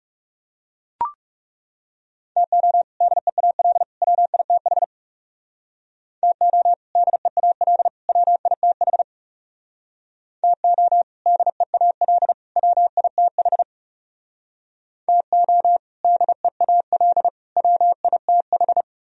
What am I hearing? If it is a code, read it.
Morse code, heard as a keyed tone, about 39 wpm, 700 Hz. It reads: TO DEAL WITH TO DEAL WITH TO DEAL WITH TO DEAL WITH